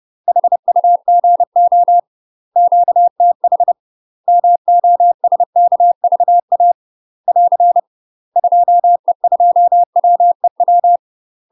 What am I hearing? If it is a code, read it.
Morse code, heard as a keyed tone, about 30 wpm, 700 Hz. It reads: HUGO QTH MOSKVA <AR> 2E2WEW
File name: UK MASTER QSOs TRAINING / 30WPM / UK MASTER QSOs 0036